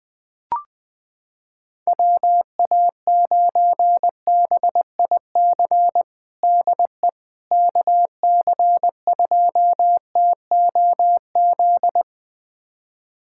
Morse code, heard as a keyed tone, about 21 wpm, 700 Hz. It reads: WA9BIC DE KC2TOZ
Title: USA Callsign DE Callsign 20wpm